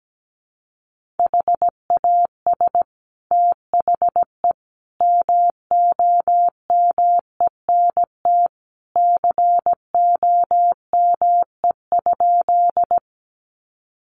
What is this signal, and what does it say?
Morse code, heard as a keyed tone, about 17 wpm, 700 Hz. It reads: HAS THE MOMENT COME?
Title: Sentences from Top 1000 Words 17wpm